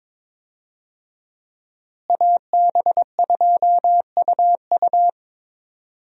Morse code, heard as a keyed tone, about 22 wpm, 700 Hz. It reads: AB2UU
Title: USA Call Signs - 5 Characters - Repeated 3x - 22wpm